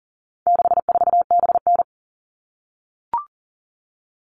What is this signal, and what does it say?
Morse code, heard as a keyed tone, about 40 wpm, 700 Hz. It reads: 64BN
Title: All Cut Numbers - 4-Digit 40wpm